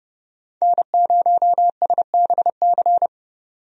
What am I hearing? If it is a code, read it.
Morse code, heard as a keyed tone, about 31 wpm, 700 Hz. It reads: N0SBC
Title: USA Call Signs - 5 Characters - 30wpm